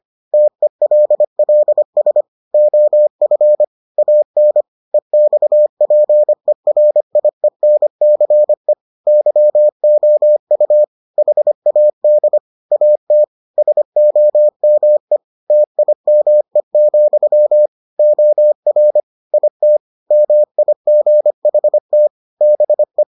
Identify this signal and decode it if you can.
Morse code, heard as a keyed tone, about 25 wpm, 600 Hz. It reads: TELLS OF AN EXPERIENCE YOU HAD AT SOME TIME, OR IT MIGHT BE